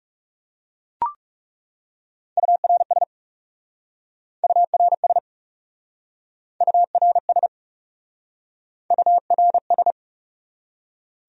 Morse code, heard as a keyed tone, about 40 wpm, 700 Hz. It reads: URS URS URS URS